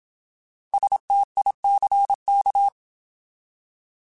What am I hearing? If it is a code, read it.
Morse code, heard as a keyed tone, about 27 wpm, 800 Hz. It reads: STICK
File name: Alphabetical / Alphabetical mp3s / Ss @25